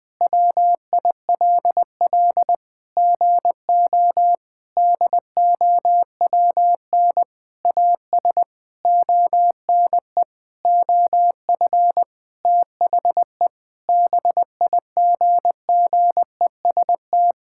Morse code, heard as a keyed tone, about 20 wpm, 700 Hz. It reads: WILL GO DOWN AS ONE OF THE BIGGEST
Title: Sentences with 7-Word Phrases - Spoken Beginning - 20wpm